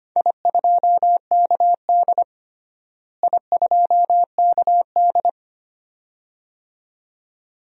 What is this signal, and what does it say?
Morse code, heard as a keyed tone, about 25 wpm, 700 Hz. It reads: I2KD I2KD
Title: DXCC-100 Call Signs - 4 Characters - Repeated 2x - 25wpm